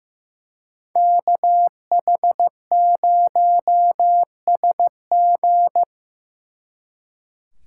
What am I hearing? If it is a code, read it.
Morse code, heard as a keyed tone, about 18 wpm, 700 Hz. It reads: KH0SG
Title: Speed Racing - USA 2x2 Call Signs 15wpm